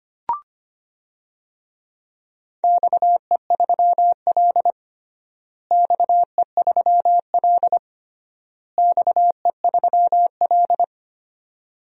Morse code, heard as a keyed tone, about 25 wpm, 700 Hz. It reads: XE3L XE3L XE3L
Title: DXCC-100 Call Signs - 4 Characters - Repeated 3x - 25wpm